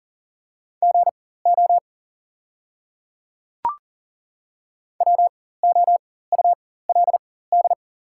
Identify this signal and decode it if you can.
Morse code, heard as a keyed tone, about 40 wpm, spaced as for 19 wpm, 700 Hz. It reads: GO WOULD
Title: 4x Character Spacing - Top 100 Words - 40wpm